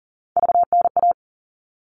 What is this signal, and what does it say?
Morse code, heard as a keyed tone, about 40 wpm, 700 Hz. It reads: VNA
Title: All Cut Numbers - 3-Digit 40wpm